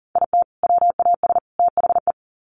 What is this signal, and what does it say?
Morse code, heard as a keyed tone, about 40 wpm, 700 Hz. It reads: IT WAS THE
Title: Sets of 3 Words 40wpm